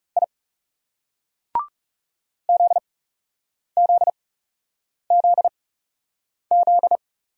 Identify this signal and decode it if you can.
Morse code, heard as a keyed tone, about 39 wpm, spaced as for 8 wpm, 700 Hz. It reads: I ZZZZ